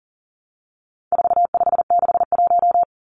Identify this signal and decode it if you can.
Morse code, heard as a keyed tone, about 40 wpm, 700 Hz. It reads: 4561
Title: Numbers - 4-Digit 40wpm